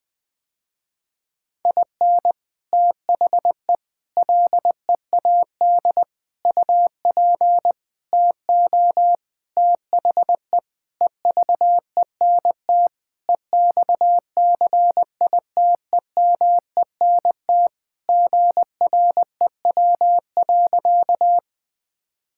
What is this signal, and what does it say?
Morse code, heard as a keyed tone, about 20 wpm, 700 Hz. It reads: IN THE LEAD UP TO THE EVENT EXCITEMENT GREW.
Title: Sentences with 6-Word Phrases - Spoken Beginning - 20wpm